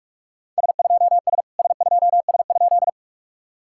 Morse code, heard as a keyed tone, about 41 wpm, 690 Hz. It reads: S2S S2S?
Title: SOTA QSO Practice 45wpm